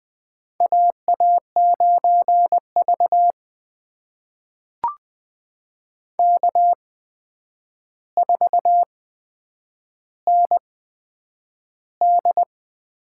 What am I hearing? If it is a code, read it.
Morse code, heard as a keyed tone, about 20 wpm, spaced as for 9 wpm, 700 Hz. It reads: AA9V K4ND
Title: USA Call Signs - 4 Characters - 8x Character Spacing - 20wpm